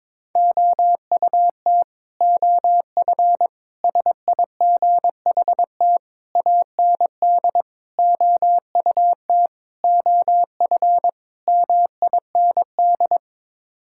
Morse code, heard as a keyed tone, about 22 wpm, 700 Hz. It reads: OUT OF SIGHT AND OUT OF MIND
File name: practice / 22wpm / Sets of 7 Words - 22wpm